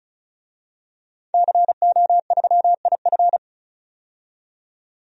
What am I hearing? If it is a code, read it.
Morse code, heard as a keyed tone, about 36 wpm, 700 Hz. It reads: CO3IF
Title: DXCC-100 Call Signs - 5 Characters - 35wpm